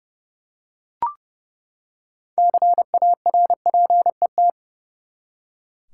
Morse code, heard as a keyed tone, about 30 wpm, 700 Hz. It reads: CARPET
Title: Six Letter Words - 30wpm